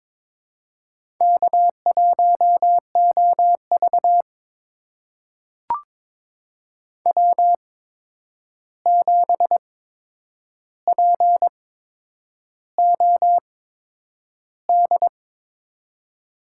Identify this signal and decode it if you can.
Morse code, heard as a keyed tone, about 22 wpm, spaced as for 9 wpm, 700 Hz. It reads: K1OV W7POD
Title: USA Call Signs - 8x Character Spacing - 22wpm